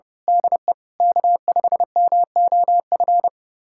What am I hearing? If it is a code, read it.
Morse code, heard as a keyed tone, about 30 wpm, 700 Hz. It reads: DE K5MOF
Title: USA Callsign DE Callsign 30wpm